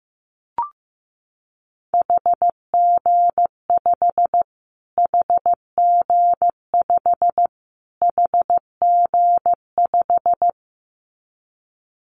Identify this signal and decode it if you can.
Morse code, heard as a keyed tone, about 14 wpm, 700 Hz. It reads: HG5 HG5 HG5
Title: Callsign Prefixes for Top 25 DXCC - Repeated 3x - 15wpm